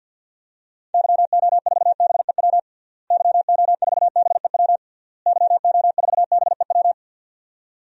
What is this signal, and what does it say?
Morse code, heard as a keyed tone, about 48 wpm, 700 Hz. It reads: YO4BEW YO4BEW YO4BEW
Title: DXCC-100 Call Signs - 6 Characters - Repeated 3x - 50wpm